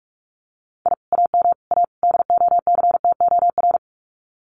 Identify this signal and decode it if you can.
Morse code, heard as a keyed tone, about 46 wpm, 700 Hz. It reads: I AM A DOCTOR
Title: Common QSO Phrases Encore 45wpm